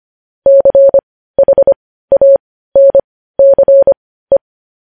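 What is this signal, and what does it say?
Morse code, heard as a keyed tone, about 25 wpm, spaced as for 15 wpm, 550 Hz. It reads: CHANCE
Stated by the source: Morse code news headlines at 15 wpm.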